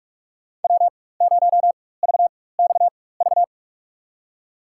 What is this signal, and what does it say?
Morse code, heard as a keyed tone, about 45 wpm, spaced as for 21 wpm, 700 Hz. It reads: W0VXV
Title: USA Call Signs - 4x Character Spacing - 45wpm